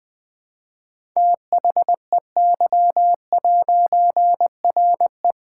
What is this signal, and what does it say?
Morse code, heard as a keyed tone, about 21 wpm, 700 Hz. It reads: THEY'RE
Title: All-Age Appropriate Jokes - 20wpm